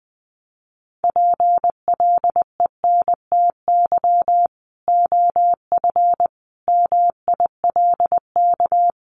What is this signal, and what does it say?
Morse code, heard as a keyed tone, about 20 wpm, 700 Hz. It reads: PLENTY OF MILK
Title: Sentences from Top 1500 Words 20wpm